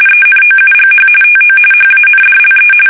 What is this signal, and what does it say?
RTTY, about 45.45 baud. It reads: ENOUGH TO HANDLE
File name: fsk850